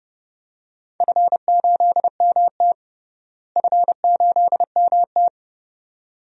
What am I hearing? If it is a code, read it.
Morse code, heard as a keyed tone, about 30 wpm, 700 Hz. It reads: F8MT F8MT
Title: DXCC-100 Call Signs - 4 Characters - Repeated 2x - 30wpm